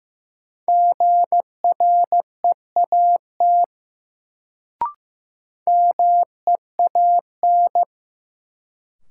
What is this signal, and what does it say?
Morse code, heard as a keyed tone, about 15 wpm, 700 Hz. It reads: GREAT MEAN
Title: Sentences from Top 200 Words - New Word Drill - 15wpm